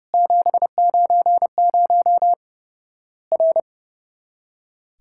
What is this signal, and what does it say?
Morse code, heard as a keyed tone, about 29 wpm, 700 Hz. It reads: 790 R
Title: Worked All Europe DX Contest - QTC Exchange - 30wpm